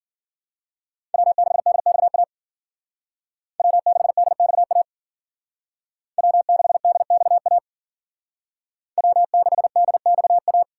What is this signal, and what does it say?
Morse code, heard as a keyed tone, about 52 wpm, 700 Hz. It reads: W6DXA W6DXA W6DXA W6DXA